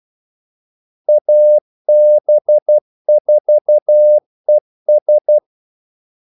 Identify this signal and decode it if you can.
Morse code, heard as a keyed tone, about 12 wpm, 600 Hz. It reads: AB4ES
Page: 2 ekstra ordmellemrum mellem kaldesignalerne